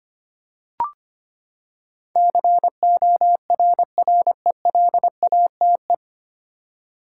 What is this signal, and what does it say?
Morse code, heard as a keyed tone, about 25 wpm, 700 Hz. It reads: CORRELATE
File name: Nine Letter Words - 25wpm